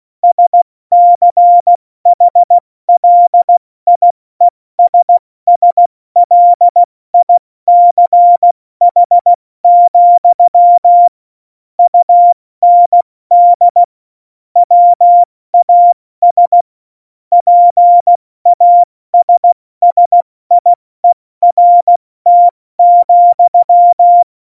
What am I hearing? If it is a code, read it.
Morse code, heard as a keyed tone, about 16 wpm, 700 Hz. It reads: SCHLIESSLICH, UND WAS PASSIERT,
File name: Morsekurs / Witze / Witz 20. November 2018, 70 F80